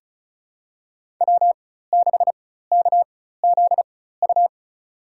Morse code, heard as a keyed tone, about 34 wpm, spaced as for 16 wpm, 700 Hz. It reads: W6KZU
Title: USA Call Signs - 4x Character Spacing - 35wpm